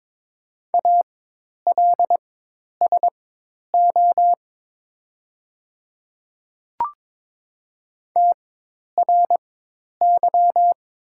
Morse code, heard as a keyed tone, about 22 wpm, spaced as for 10 wpm, 700 Hz. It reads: ALSO TRY
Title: 4x Character Spacing - Top 500 Words - 22wpm